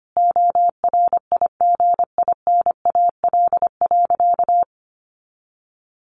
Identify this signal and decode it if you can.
Morse code, heard as a keyed tone, about 25 wpm, 700 Hz. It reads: ORIGINAL.
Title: Sentences from Top 900 Words 25wpm